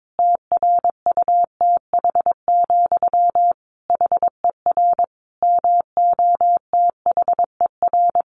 Morse code, heard as a keyed tone, about 22 wpm, 700 Hz. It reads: TRUTH, HER MOTHER
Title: Sentences from Top 800 Words 22wpm